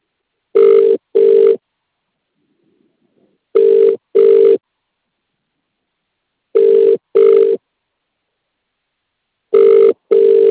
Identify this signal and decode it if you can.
Ringback tone (UK), 400 + 450 Hz.